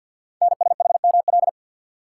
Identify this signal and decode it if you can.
Morse code, heard as a keyed tone, about 48 wpm, 690 Hz. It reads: NSHML
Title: Code Groups - 5 Letters - 50wpm